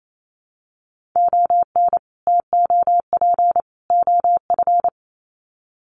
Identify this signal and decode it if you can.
Morse code, heard as a keyed tone, about 28 wpm, 700 Hz. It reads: ON TOP OF
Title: Sets of 3 Words 28wpm